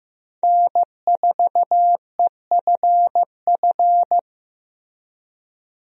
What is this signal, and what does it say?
Morse code, heard as a keyed tone, about 15 wpm, 700 Hz. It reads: N4EFF
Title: USA Call Signs - 5 Characters - Repeated 3x - 15wpm